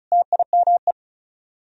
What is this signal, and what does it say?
Morse code, heard as a keyed tone, about 46 wpm, 700 Hz. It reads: TIME